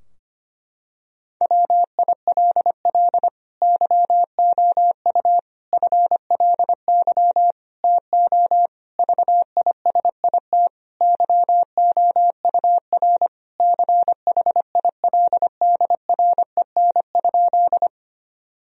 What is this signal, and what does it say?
Morse code, heard as a keyed tone, about 25 wpm, 700 Hz. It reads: WILL YOU FLY TO VISIT YOUR CHILDREN?
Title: Sentences from Top 1000 Words - Review Part 2 - 25wpm